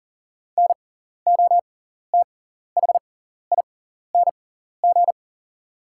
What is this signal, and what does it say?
Morse code, heard as a keyed tone, about 39 wpm, spaced as for 14 wpm, 700 Hz. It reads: NOTHING
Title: 6x Character Spacing - Top 500 Words - 40wpm